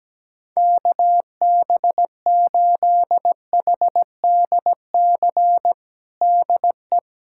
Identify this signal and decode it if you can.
Morse code, heard as a keyed tone, about 17 wpm, 700 Hz. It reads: KB8HDC DE
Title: USA Callsign DE Callsign - 6 Characters - 17wpm